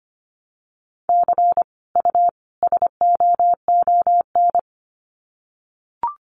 Morse code, heard as a keyed tone, about 25 wpm, 700 Hz. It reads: C U SOON
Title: Common QSO Phrases 25wpm